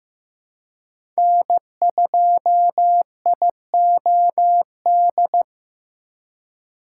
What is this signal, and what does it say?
Morse code, heard as a keyed tone, about 15 wpm, 700 Hz. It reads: N2IOD
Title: USA Call Signs - 2x Character Spacing - 15wpm